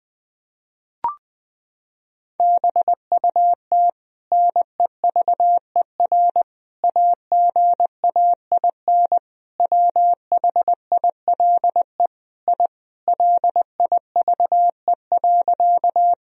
Morse code, heard as a keyed tone, about 21 wpm, 700 Hz. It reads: BUT NEVER AGAIN WHILE I LIVE.
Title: Sentences from Top 1000 Words - Review Part 1 - 20wpm